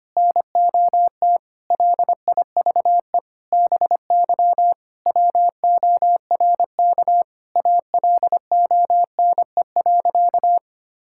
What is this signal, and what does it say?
Morse code, heard as a keyed tone, about 25 wpm, 700 Hz. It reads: NOT LIVE BY WORK ALONE.
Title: Sentences from Top 1600 Words 25wpm